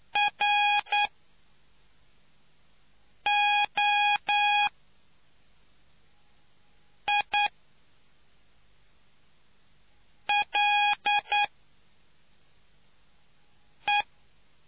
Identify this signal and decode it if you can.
Morse code, heard as a keyed tone, about 9 wpm, spaced as for 3 wpm, 820 Hz. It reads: ROILE